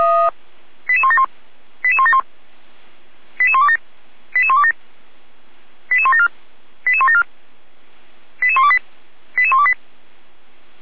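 Five-tone selective call (ZVEI), heard as quick runs of five tones, tones of about 80 ms. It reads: 88171 88171 88127 88127 88174 88174 88128 88128
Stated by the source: ZVEI five-tone selective call received by SDR, fire brigade Feuerwehr Orlamünde.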